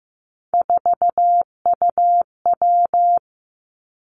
Morse code, heard as a keyed tone, about 15 wpm, 700 Hz. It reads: 4UW
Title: Three Challenging Letter-Number Combinations 15wpm